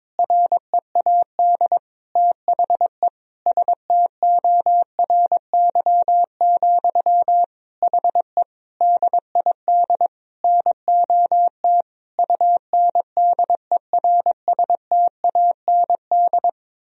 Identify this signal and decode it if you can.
Morse code, heard as a keyed tone, about 21 wpm, 700 Hz. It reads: READ THE STORY, HE DID NOT UNDERSTAND
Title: Sentences from Top 1000 Words - Review Part 3 - 22wpm